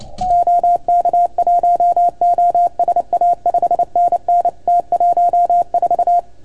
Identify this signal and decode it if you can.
Morse code, heard as a keyed tone, about 28 wpm, 690 Hz. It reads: OK1OSA 5NN T14
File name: Archiv / CUC310 / OK1FGD CUC 310 pileup